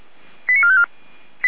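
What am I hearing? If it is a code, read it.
Five-tone selective call (ZVEI), heard as quick runs of five tones, tones of about 70 ms. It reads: 89454
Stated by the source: ZVEI five-tone selective call received on SDR.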